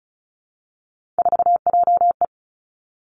Morse code, heard as a keyed tone, about 36 wpm, 700 Hz. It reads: 4JE
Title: Three Challenging Letter-Number Combinations 35wpm